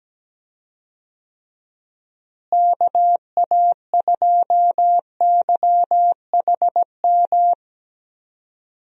Morse code, heard as a keyed tone, about 17 wpm, 700 Hz. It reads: KA2YHM
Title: USA Call Signs - 6 Characters - Repeated 3x - 17wpm